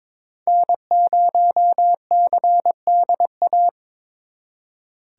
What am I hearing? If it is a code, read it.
Morse code, heard as a keyed tone, about 26 wpm, 700 Hz. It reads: N0CDA
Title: Speed Racing - USA 1x3 Call Signs 22wpm